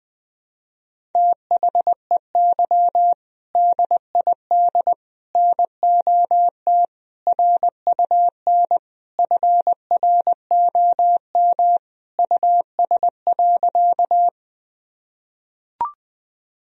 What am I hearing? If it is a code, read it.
Morse code, heard as a keyed tone, about 20 wpm, 700 Hz. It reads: THEY DID NOT RUN FROM US.
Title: Sentences from Top 1000 Words - Review Part 1 - 20wpm